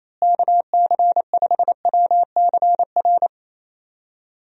Morse code, heard as a keyed tone, about 28 wpm, 700 Hz. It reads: KC5WCR
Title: USA Call Signs - 6 Characters - Repeated 2x - 28wpm